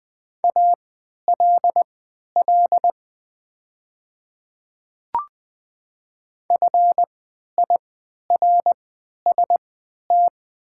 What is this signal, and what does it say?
Morse code, heard as a keyed tone, about 20 wpm, spaced as for 11 wpm, 690 Hz. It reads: ALL FIRST
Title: 3x Character Spacing - Top 100 Words - 20wpm